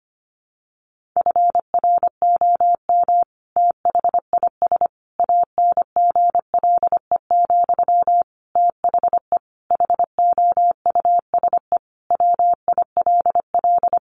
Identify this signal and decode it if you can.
Morse code, heard as a keyed tone, about 25 wpm, 700 Hz. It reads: FROM THIS ANGLE, THE HOUSE WILL
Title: Sentences from Top 1500 Words 25wpm